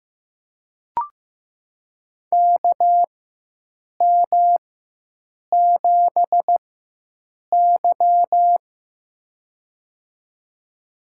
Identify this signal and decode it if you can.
Morse code, heard as a keyed tone, about 15 wpm, spaced as for 7 wpm, 700 Hz. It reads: KM7Y
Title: USA Call Signs - 4x Character Spacing - 15wpm